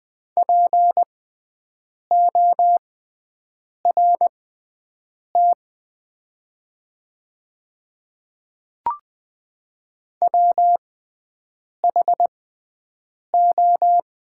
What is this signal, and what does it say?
Morse code, heard as a keyed tone, about 20 wpm, spaced as for 7 wpm, 700 Hz. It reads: PORT WHO
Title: 6x Character Spacing - Top 300 Words - 20wpm